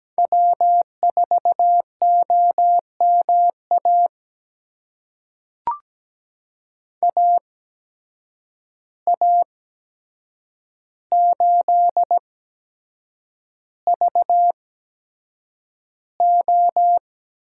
Morse code, heard as a keyed tone, about 17 wpm, spaced as for 7 wpm, 700 Hz. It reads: W4OMA AA8VO
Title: USA Call Signs - 5 Characters - 8x Character Spacing - 17wpm